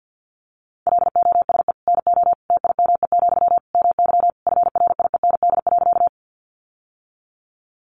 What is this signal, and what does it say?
Morse code, heard as a keyed tone, about 50 wpm, 700 Hz. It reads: LOSE NO TIME, MY FRIEND.
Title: Sentences from Top 1100 Words 50wpm